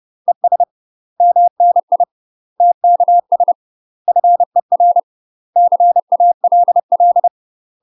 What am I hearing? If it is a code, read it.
Morse code, heard as a keyed tone, about 30 wpm, 700 Hz. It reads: ES MNI TKS FER CALL
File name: UK MASTER QSOs TRAINING / 30WPM / UK MASTER QSOs 0004